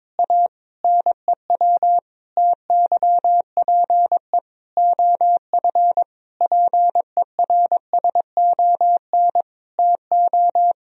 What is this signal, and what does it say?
Morse code, heard as a keyed tone, about 22 wpm, 700 Hz. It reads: A NEW TYPE OF PERSON TO
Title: Sentences from Top 1000 Words - Review Part 3 - 22wpm